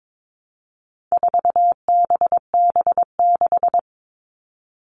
Morse code, heard as a keyed tone, about 22 wpm, 700 Hz. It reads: 4BB6
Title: All Cut Numbers - 4-Digit 22wpm